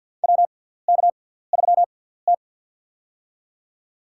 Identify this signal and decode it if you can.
Morse code, heard as a keyed tone, about 50 wpm, spaced as for 17 wpm, 700 Hz. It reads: WK3T
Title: USA Call Signs - 4 Characters - 6x Character Spacing - 50wpm